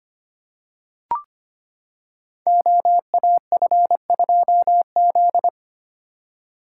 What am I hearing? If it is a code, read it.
Morse code, heard as a keyed tone, about 25 wpm, 700 Hz. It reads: OAF2Z
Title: Code Groups - 5 Letter:Numbers - 25wpm